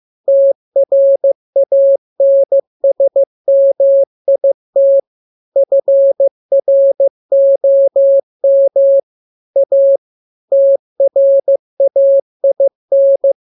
Morse code, heard as a keyed tone, about 15 wpm, 550 Hz. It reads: TRANSMIT FROM A TRAIN
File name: jokes / joke 20 15 0000